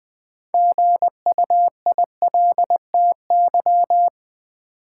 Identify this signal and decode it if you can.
Morse code, heard as a keyed tone, about 20 wpm, 700 Hz. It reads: GUILTY
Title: Six Letter Words - 20wpm